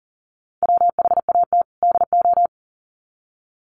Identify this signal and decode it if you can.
Morse code, heard as a keyed tone, about 40 wpm, 700 Hz. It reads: WHAT DO
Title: Sets of 2 Words 40wpm